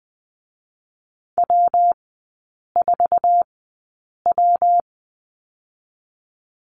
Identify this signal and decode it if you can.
Morse code, heard as a keyed tone, about 20 wpm, 700 Hz. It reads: W 4 W
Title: 114 Lesson 19e - Letter w - Three Characters 20wpm